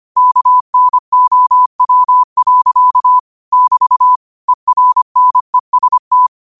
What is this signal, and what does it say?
Morse code, heard as a keyed tone, about 25 wpm, 1000 Hz. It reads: KNOW. <BT> ERNEST
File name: qotdfiles / 25wpm